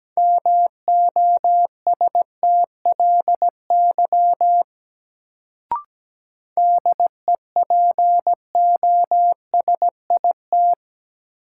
Morse code, heard as a keyed tone, about 17 wpm, 700 Hz. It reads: MOSTLY DEPOSIT
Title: Sentences from Top 1300 Words - New Word Drill - 17wpm